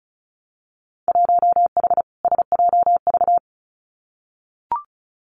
Morse code, heard as a keyed tone, about 36 wpm, 700 Hz. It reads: 1H SJV
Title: Winter Field Day Exchanges 35wpm